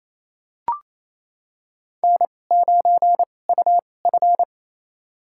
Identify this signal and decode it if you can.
Morse code, heard as a keyed tone, about 27 wpm, spaced as for 20 wpm, 700 Hz. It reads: N9UF
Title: USA Call Signs - 4 Characters - 2x Character Spacing - 28wpm